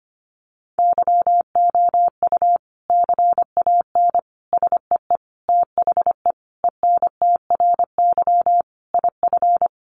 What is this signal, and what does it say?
Morse code, heard as a keyed tone, about 25 wpm, 700 Hz. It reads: YOU CAN SEE THE ENTRY IF
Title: Sentences from Top 1500 Words 25wpm